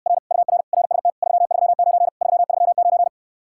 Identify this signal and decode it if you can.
Morse code, heard as a keyed tone, about 69 wpm, 700 Hz. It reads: R UR RST 339 339